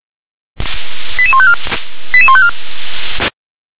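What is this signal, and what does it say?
Five-tone selective call (ZVEI), heard as quick runs of five tones, tones of about 70 ms. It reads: 88154 88154
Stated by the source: ZVEI five-tone selective call received on an SDR.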